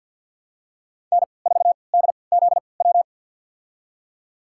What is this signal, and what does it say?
Morse code, heard as a keyed tone, about 51 wpm, spaced as for 28 wpm, 700 Hz. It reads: N4DZW
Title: USA Call Signs - 5 Characters - 3x Character Spacing - 50wpm